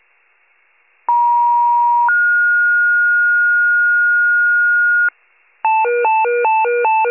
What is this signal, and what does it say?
Two-tone page, heard as a one-second tone, then a three-second tone, about 950 then 1430 Hz, with noise. It is heard from 1.1 to 5.1 s.
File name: pager / 83 FIRE / wcalert